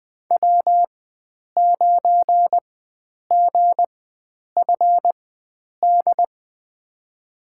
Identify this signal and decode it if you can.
Morse code, heard as a keyed tone, about 20 wpm, spaced as for 9 wpm, 700 Hz. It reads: W9GFD
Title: USA Call Signs - 5 Characters - 4x Character Spacing - 20wpm